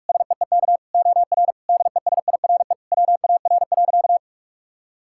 Morse code, heard as a keyed tone, about 43 wpm, 700 Hz. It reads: SEEK OR DESIRE WAR.
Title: Sentences from Top 1600 Words 45wpm